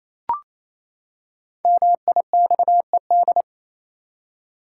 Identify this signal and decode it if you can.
Morse code, heard as a keyed tone, about 28 wpm, 700 Hz. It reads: MIXED
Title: Five Letter Words - 28wpm